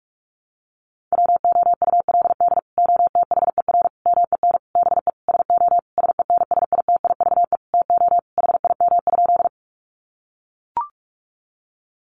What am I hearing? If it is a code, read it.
Morse code, heard as a keyed tone, about 44 wpm, 700 Hz. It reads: WOULD OTHER MEN BE SO SENSITIVE TO HIM?
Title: Sentences from Top 1400 Words 45wpm